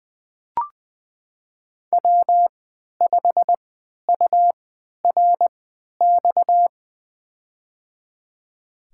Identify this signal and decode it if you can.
Morse code, heard as a keyed tone, about 21 wpm, spaced as for 11 wpm, 700 Hz. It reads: W5URX
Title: USA Call Signs - 5 Characters - 3x Character Spacing - 20wpm